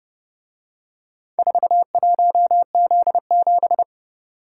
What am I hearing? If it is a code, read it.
Morse code, heard as a keyed tone, about 30 wpm, 700 Hz. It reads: 41Z7